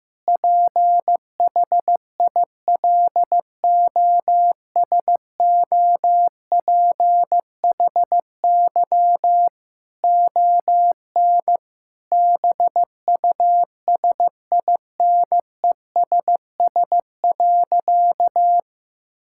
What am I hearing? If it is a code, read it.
Morse code, heard as a keyed tone, about 15 wpm, 700 Hz. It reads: PHILOSOPHY ON BUSINESS.
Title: Sentences from Top 1000 Words - Review Part 6 - 15wpm